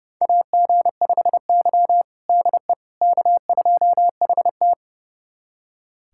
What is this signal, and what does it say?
Morse code, heard as a keyed tone, about 31 wpm, 700 Hz. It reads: AG5Y DE K2HT
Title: USA Callsign DE Callsign - 4 Characters - 30wpm